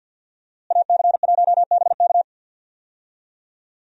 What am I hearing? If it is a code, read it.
Morse code, heard as a keyed tone, about 50 wpm, 700 Hz. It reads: AC1BK
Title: USA Call Signs - 5 Characters - Repeated 3x - 50wpm